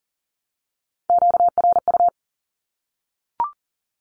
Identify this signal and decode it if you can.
Morse code, heard as a keyed tone, about 40 wpm, 700 Hz. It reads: QRU
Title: Q-Codes Repeated 3x 40wpm